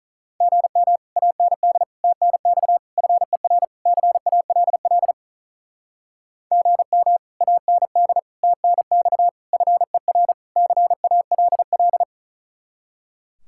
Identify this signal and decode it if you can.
Morse code, heard as a keyed tone, about 43 wpm, 700 Hz. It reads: GM AND TNX FER CALL GM AND TNX FER CALL
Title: Speed-Racing Common QSO Phrases - 35wpm